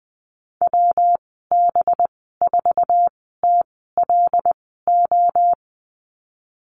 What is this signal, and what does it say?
Morse code, heard as a keyed tone, about 20 wpm, spaced as for 15 wpm, 700 Hz. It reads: WB4TLO
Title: 186 Lesson 30g - Letter b - Call Signs 20wpm